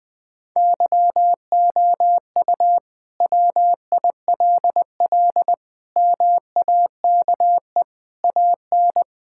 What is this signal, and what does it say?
Morse code, heard as a keyed tone, about 20 wpm, 700 Hz. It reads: YOU WILL MAKE AN
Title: Sentences from Top 1000 Words - Review Part 5 - 20wpm